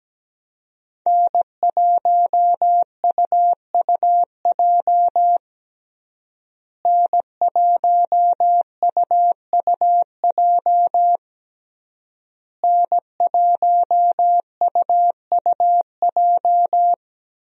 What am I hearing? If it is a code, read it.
Morse code, heard as a keyed tone, about 17 wpm, 700 Hz. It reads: N1UUJ N1UUJ N1UUJ
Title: USA Call Signs - 5 Characters - Repeated 3x - 17wpm